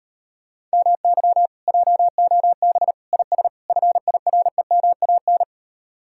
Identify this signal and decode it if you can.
Morse code, heard as a keyed tone, about 38 wpm, 700 Hz. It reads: MY JOB IS FIREMAN